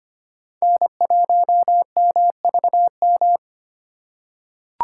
Morse code, heard as a keyed tone, about 25 wpm, 700 Hz. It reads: N1MVM
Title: USA Call Signs - 5 Characters - 2x Character Spacing - 25wpm